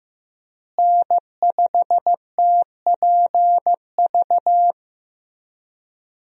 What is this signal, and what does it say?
Morse code, heard as a keyed tone, about 15 wpm, 700 Hz. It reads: N5TPV
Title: USA Call Signs - 5 Characters - Repeated 2x - 15wpm